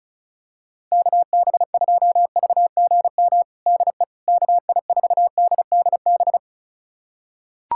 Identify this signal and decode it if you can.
Morse code, heard as a keyed tone, about 36 wpm, 700 Hz. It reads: KB2VGM DE KI4DDB
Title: USA Callsign DE Callsign - 6 Characters - 35wpm